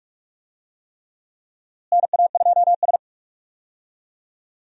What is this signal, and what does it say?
Morse code, heard as a keyed tone, about 45 wpm, 700 Hz. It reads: NA2S
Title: USA Call Signs - 4 Characters - Repeated 3x - 45wpm